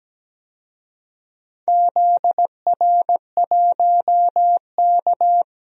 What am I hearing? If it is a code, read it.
Morse code, heard as a keyed tone, about 17 wpm, 700 Hz. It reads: ZR1K
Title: DXCC-100 Call Signs - 4 Characters - Repeated 2x - 17wpm